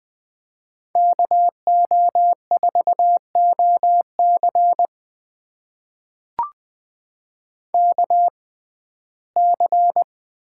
Morse code, heard as a keyed tone, about 20 wpm, spaced as for 14 wpm, 700 Hz. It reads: KO4OC KC
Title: USA Call Signs - 5 Characters - 6x Character Spacing - 20wpm